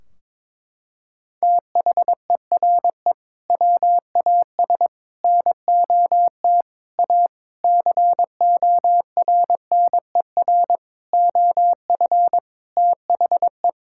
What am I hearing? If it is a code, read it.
Morse code, heard as a keyed tone, about 22 wpm, 700 Hz. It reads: THERE WAS NOT A CORNER OF THE
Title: Sentences from Top 1600 Words 22wpm